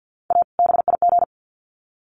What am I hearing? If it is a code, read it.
Morse code, heard as a keyed tone, about 50 wpm, 700 Hz. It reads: A BIG
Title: Sets of 2 Words 50wpm